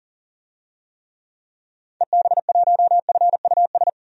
Morse code, heard as a keyed tone, about 40 wpm, 700 Hz. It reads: EB1FUS
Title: DXCC-100 Call Signs - 6 Characters - 40wpm